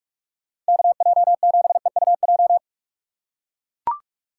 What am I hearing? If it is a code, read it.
Morse code, heard as a keyed tone, about 44 wpm, 700 Hz. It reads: KJ7EUJ